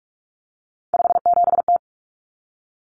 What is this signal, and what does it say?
Morse code, heard as a keyed tone, about 44 wpm, 700 Hz. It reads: 57T
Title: Common Cut Numbers - 3-Digit 45wpm